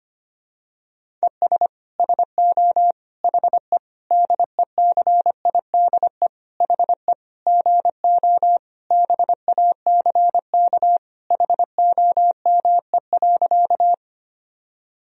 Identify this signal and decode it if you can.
Morse code, heard as a keyed tone, about 25 wpm, 700 Hz. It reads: ES SO HE DECIDE HE GO BACK HOME.
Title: Sentences from Top 500 Words and Abbreviations 25wpm